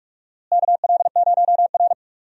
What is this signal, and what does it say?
Morse code, heard as a keyed tone, about 47 wpm, 700 Hz. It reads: KL0R
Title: USA Call Signs - 4 Characters - Repeated 2x - 45wpm